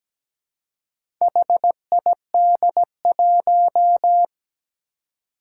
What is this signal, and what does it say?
Morse code, heard as a keyed tone, about 17 wpm, 700 Hz. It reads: HID1
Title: Four Challenging Letter-Number Combinations 17wpm